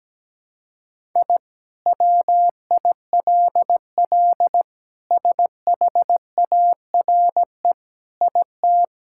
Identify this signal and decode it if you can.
Morse code, heard as a keyed tone, about 17 wpm, 700 Hz. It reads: I WILL SHARE IT
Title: Sentences from Top 1000 Words - Review Part 6 - 17wpm